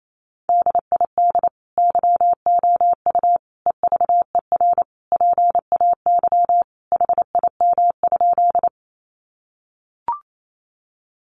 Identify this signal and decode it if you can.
Morse code, heard as a keyed tone, about 28 wpm, 700 Hz. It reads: DID YOU EVER PAY HIM?
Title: Sentences from Top 700 Words 28wpm